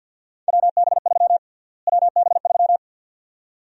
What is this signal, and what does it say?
Morse code, heard as a keyed tone, about 52 wpm, 700 Hz. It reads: WB3 WB3
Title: USA Call Sign Prefixes - Repeated 2x - 50wpm